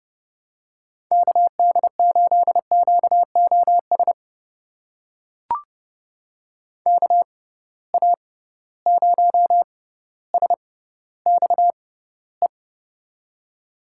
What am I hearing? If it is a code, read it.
Morse code, heard as a keyed tone, about 30 wpm, spaced as for 15 wpm, 700 Hz. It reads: KD8QOS KA0SXE
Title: USA Call Signs - 6 Characters - 6x Character Spacing - 30wpm